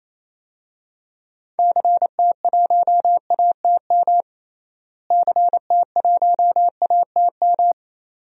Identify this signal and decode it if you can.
Morse code, heard as a keyed tone, about 27 wpm, 700 Hz. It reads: CT1ATM CT1ATM
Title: DXCC-100 Call Signs - 6 Characters - Repeated 2x - 28wpm